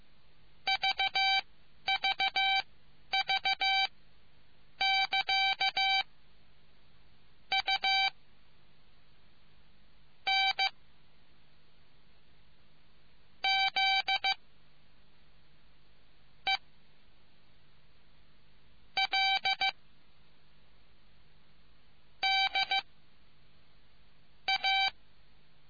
Morse code, heard as a keyed tone, about 15 wpm, spaced as for 11 wpm, 770 Hz. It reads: VVV <KA> U N Z E L D A